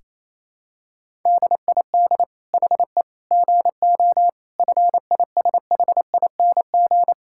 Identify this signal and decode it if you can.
Morse code, heard as a keyed tone, about 28 wpm, 700 Hz. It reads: DID HE GO FISHING
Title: Sentences from Top 1000 Words - Review Part 5 - 28wpm